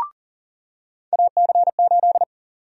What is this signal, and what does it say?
Morse code, heard as a keyed tone, about 40 wpm, 700 Hz. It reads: AC8
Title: USA Call Sign Prefixes - 40wpm